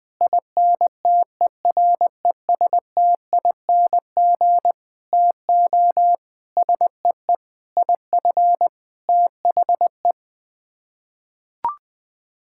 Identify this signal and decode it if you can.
Morse code, heard as a keyed tone, about 21 wpm, 700 Hz. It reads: INTERESTING TO SEE IF THE
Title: Sets of 8 Words - 20wpm